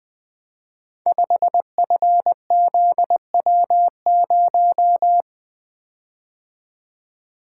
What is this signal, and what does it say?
Morse code, heard as a keyed tone, about 20 wpm, 700 Hz. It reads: 5FZW0
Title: Code Groups - 5 Letter:Numbers - 20wpm